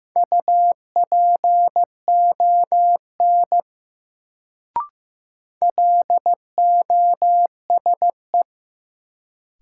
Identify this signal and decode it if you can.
Morse code, heard as a keyed tone, about 15 wpm, 700 Hz. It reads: UPON LOSE
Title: Sentences from Top 1100 Words - New Word Drill Alt - 15wpm